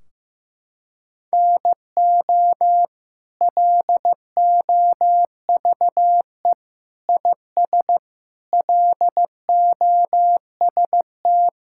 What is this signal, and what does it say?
Morse code, heard as a keyed tone, about 15 wpm, 700 Hz. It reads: NO LOVE IS LOST
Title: Sentences from Top 1600 Words 15wpm